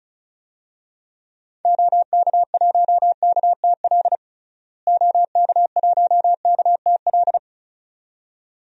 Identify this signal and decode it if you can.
Morse code, heard as a keyed tone, about 36 wpm, 700 Hz. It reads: OK1KTL OK1KTL
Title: DXCC-100 Call Signs - 6 Characters - Repeated 2x - 35wpm